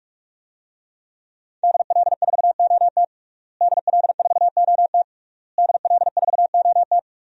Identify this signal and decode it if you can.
Morse code, heard as a keyed tone, about 46 wpm, 700 Hz. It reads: DL4OT DL4OT DL4OT
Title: USA Call Signs - 5 Characters - Repeated 3x - 45wpm